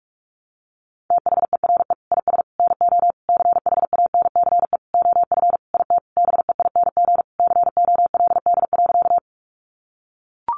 Morse code, heard as a keyed tone, about 44 wpm, 700 Hz. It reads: THERE IS NO CHANCE OF IT BEING COLD.
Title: Sentences from Top 800 Words 45wpm